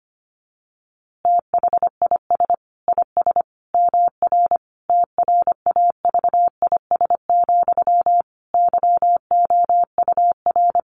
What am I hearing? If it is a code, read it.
Morse code, heard as a keyed tone, about 25 wpm, 700 Hz. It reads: THIS IS MR TRAVIS, YOUR
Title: A Sound Of Thunder 25wpm